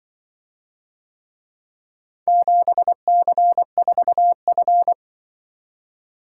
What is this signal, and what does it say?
Morse code, heard as a keyed tone, about 23 wpm, 700 Hz. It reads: 7C4F
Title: Speed-Racing - Four Letter-Number Combinations - 20wpm